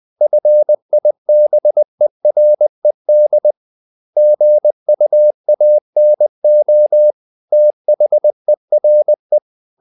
Morse code, heard as a keyed tone, about 20 wpm, 600 Hz. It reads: FIBERED GUANO THERE
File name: the old man and the sea / the old man and the sea part 1 20wpm